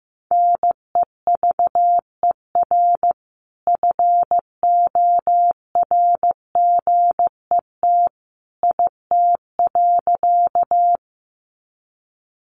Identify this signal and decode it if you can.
Morse code, heard as a keyed tone, about 15 wpm, 700 Hz. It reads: NEVER FORGET IT.
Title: Sentences from Top 1200 Words 15wpm